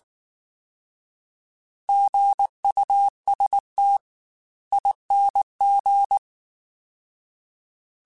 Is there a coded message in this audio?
Yes: Morse code from 1.9 to 6.2 s, heard as a keyed tone, about 19 wpm, 800 Hz. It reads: GUST ING